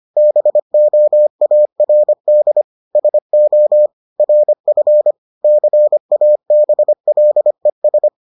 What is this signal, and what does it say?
Morse code, heard as a keyed tone, about 25 wpm, 600 Hz. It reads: BOARD SO RF CABLES